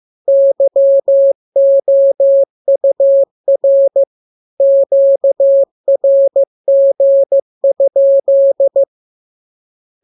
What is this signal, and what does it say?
Morse code, heard as a keyed tone, about 15 wpm, 550 Hz. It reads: YOUR QRG?
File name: jokes / joke 21 15 0000